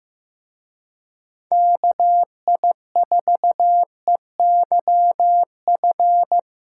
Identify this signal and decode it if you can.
Morse code, heard as a keyed tone, about 18 wpm, 700 Hz. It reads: KI4EYF
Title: Speed Racing - USA 2x3 Call Signs 15wpm